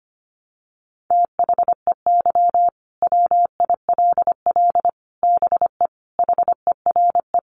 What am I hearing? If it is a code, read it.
Morse code, heard as a keyed tone, about 25 wpm, 700 Hz. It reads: THEY WILL BE HERE
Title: Sentences from Top 600 Words 25wpm